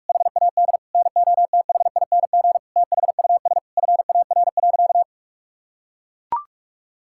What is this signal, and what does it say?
Morse code, heard as a keyed tone, about 41 wpm, 700 Hz. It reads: HAD NOTHING THUS FAR.
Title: Sentences from Top 1000 Words - Review Part 5 - 45wpm